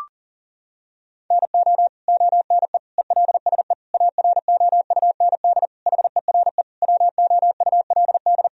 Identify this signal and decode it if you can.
Morse code, heard as a keyed tone, about 40 wpm, 700 Hz. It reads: NO ONE ELSE AROUND HERE WOULD
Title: Sentences from Top 1000 Words - Review Part 3 - 40wpm